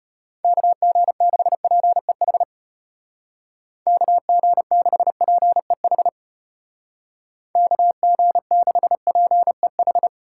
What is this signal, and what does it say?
Morse code, heard as a keyed tone, about 33 wpm, 700 Hz. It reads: KG6PEH KG6PEH KG6PEH